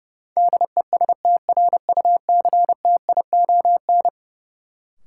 Morse code, heard as a keyed tone, about 30 wpm, 700 Hz. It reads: DESTRUCTION
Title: Sets of 5 Words Encore - New Word Drill - 30wpm